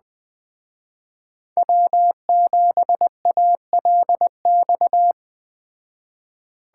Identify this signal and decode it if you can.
Morse code, heard as a keyed tone, about 20 wpm, 700 Hz. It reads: W7ALX